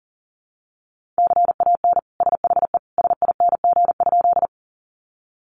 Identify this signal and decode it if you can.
Morse code, heard as a keyed tone, about 40 wpm, 700 Hz. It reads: CAN SHE SING?
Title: Sentences from Top 400 Words 40wpm